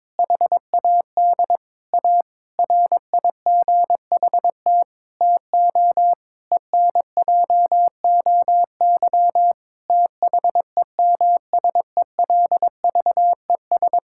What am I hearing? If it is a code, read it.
Morse code, heard as a keyed tone, about 22 wpm, 700 Hz. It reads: HAD A RIGHT TO ENJOY THEMSELVES
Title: Sentences from Top 1000 Words - Review Part 5 - 22wpm